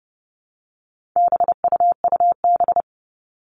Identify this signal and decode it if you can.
Morse code, heard as a keyed tone, about 30 wpm, 700 Hz. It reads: BUUB
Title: All Cut Numbers - 4-Digit 30wpm